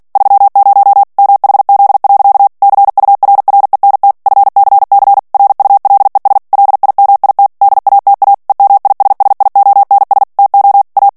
Morse code, heard as a keyed tone, about 48 wpm, 800 Hz. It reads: 30 MHZ. CURRENT FCC RULES LIMIT DATA EMISSIONS TO A
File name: ANGLAIS texte clair 3 vitesse 45 mots minute